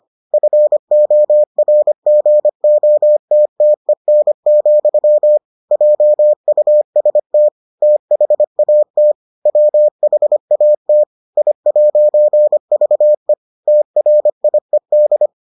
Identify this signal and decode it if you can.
Morse code, heard as a keyed tone, about 25 wpm, 600 Hz. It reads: FORGOTTEN, JUST THAT WHAT I'VE TRIED